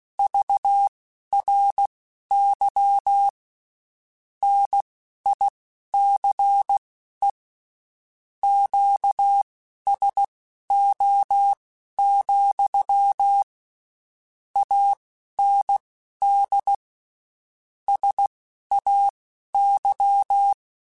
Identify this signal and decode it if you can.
Morse code, heard as a keyed tone, about 16 wpm, spaced as for 12 wpm, 800 Hz. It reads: VRY NICE QSO, AND SAY
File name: (spaced) / QSOs Spaced-mp3s / kc7sac @15 sp